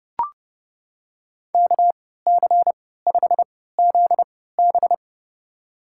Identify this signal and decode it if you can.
Morse code, heard as a keyed tone, about 30 wpm, spaced as for 17 wpm, 700 Hz. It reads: KC5ZB
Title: USA Call Signs - 5 Characters - 3x Character Spacing - 30wpm